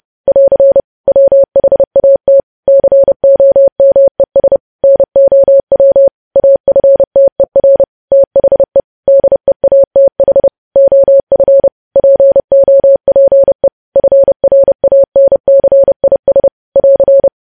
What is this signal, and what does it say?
Morse code, heard as a keyed tone, about 30 wpm, 550 Hz. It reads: <AR> WHAT COMES NOW AFTER THE DEATH OF POPE FRANCIS <AR>
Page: News Headlines in Morse Code at 30 WPM